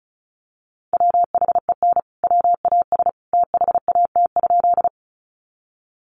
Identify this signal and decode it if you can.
Morse code, heard as a keyed tone, about 34 wpm, 700 Hz. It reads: WHEN WAS THAT?
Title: Sentences from Top 100 Words 35wpm